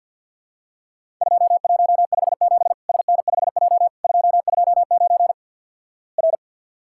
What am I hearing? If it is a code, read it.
Morse code, heard as a keyed tone, about 48 wpm, 700 Hz. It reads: 2157 SN5J 229 R
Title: Worked All Europe DX Contest - QTC Exchange - 50wpm